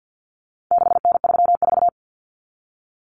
Morse code, heard as a keyed tone, about 50 wpm, 700 Hz. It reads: 6N34
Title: Common Cut Numbers - 4-Digit 50wpm